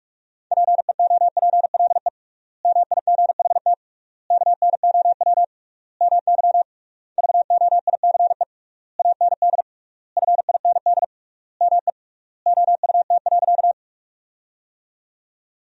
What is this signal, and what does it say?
Morse code, heard as a keyed tone, about 46 wpm, 700 Hz. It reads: PEOPLE MIGHT KNOW MY VOICE AND FIND ME OUT.
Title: 3x Word-Spacing - Sentences from Top 500 Words - 45wpm